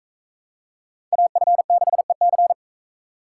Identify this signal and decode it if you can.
Morse code, heard as a keyed tone, about 44 wpm, 700 Hz. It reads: AF6EC